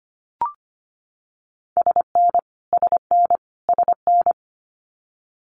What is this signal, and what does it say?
Morse code, heard as a keyed tone, about 25 wpm, 700 Hz. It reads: SN SN SN
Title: Abbreviations Repeated 3x - 25wpm